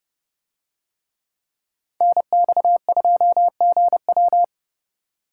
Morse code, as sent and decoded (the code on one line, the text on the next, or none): -. -..- ..--- --. .--
NX2GW